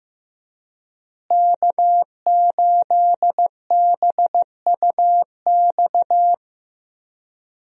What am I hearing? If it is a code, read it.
Morse code, heard as a keyed tone, about 15 wpm, 700 Hz. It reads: K8BUX
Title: USA Call Signs - 5 Characters - Repeated 3x - 15wpm